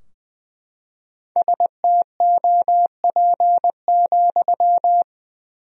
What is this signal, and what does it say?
Morse code, heard as a keyed tone, about 20 wpm, 700 Hz. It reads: STOP,
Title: 4x Word-Spacing - Sentences from Top 300 Words - 20wpm